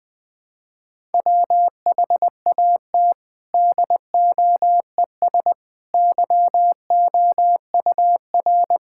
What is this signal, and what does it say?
Morse code, heard as a keyed tone, about 19 wpm, 700 Hz. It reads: WHAT DOES YOUR
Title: Sentences from Top 1000 Words - Review Part 5 - 20wpm